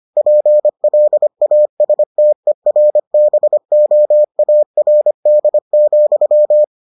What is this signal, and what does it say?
Morse code, heard as a keyed tone, about 25 wpm, 600 Hz. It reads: PLASTERBOARD,